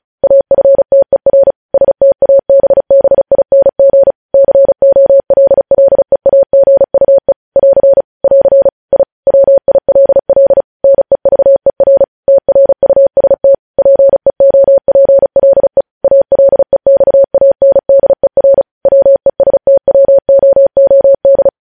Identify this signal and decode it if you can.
Morse code, heard as a keyed tone, about 36 wpm, 550 Hz. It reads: AFTER STABBING COLLEAGUE <AR> <AR> I WILL NEVER TRUST PEOPLE ALEXANDER WESTWOOD
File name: podcast / News headlines-1x-35x35